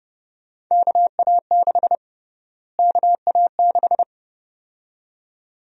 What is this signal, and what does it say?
Morse code, heard as a keyed tone, about 32 wpm, 700 Hz. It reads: KA6 KA6
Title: USA Call Sign Prefixes - Repeated 2x - 30wpm